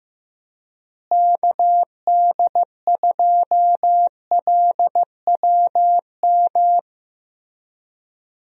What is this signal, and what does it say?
Morse code, heard as a keyed tone, about 15 wpm, 700 Hz. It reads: KD2LWM
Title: USA Call Signs - 6 Characters - Repeated 3x - 15wpm